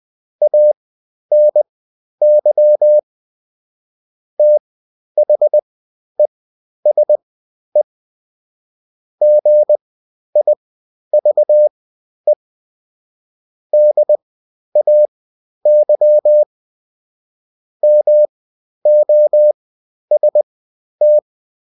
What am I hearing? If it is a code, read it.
Morse code, heard as a keyed tone, about 20 wpm, spaced as for 11 wpm, 600 Hz. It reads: ANY THESE GIVE DAY MOST
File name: Top-100-Words-10